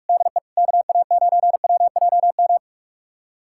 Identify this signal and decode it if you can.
Morse code, heard as a keyed tone, about 46 wpm, 700 Hz. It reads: DE KA9WJM
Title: USA Callsign DE Callsign 45wpm